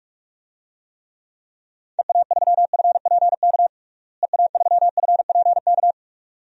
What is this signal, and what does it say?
Morse code, heard as a keyed tone, about 42 wpm, 700 Hz. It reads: EA3FPK EA3FPK
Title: DXCC-100 Call Signs - 6 Characters - Repeated 2x - 45wpm